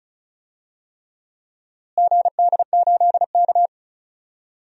Morse code, heard as a keyed tone, about 34 wpm, 700 Hz. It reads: GD8K
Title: DXCC-100 Call Signs - 4 Characters - Repeated 3x - 35wpm